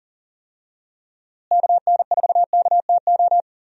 Morse code, heard as a keyed tone, about 37 wpm, 700 Hz. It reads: KN4KTO
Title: USA Call Signs - 6 Characters - Repeated 3x - 40wpm